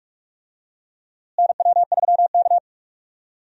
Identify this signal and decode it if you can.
Morse code, heard as a keyed tone, about 46 wpm, 700 Hz. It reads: NW3K
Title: USA Call Signs - 4 Characters - Repeated 2x - 45wpm